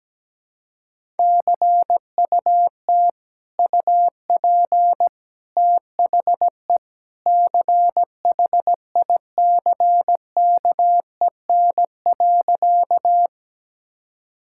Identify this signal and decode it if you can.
Morse code, heard as a keyed tone, about 17 wpm, 700 Hz. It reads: CUT UP THE CHICKEN.
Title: Sentences from Top 1000 Words - Review Part 5 - 17wpm